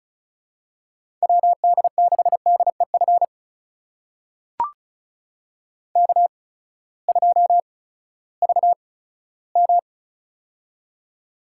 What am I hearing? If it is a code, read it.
Morse code, heard as a keyed tone, about 36 wpm, spaced as for 18 wpm, 700 Hz. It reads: WD6DEF K2VM
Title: USA Call Signs - 8x Character Spacing - 35wpm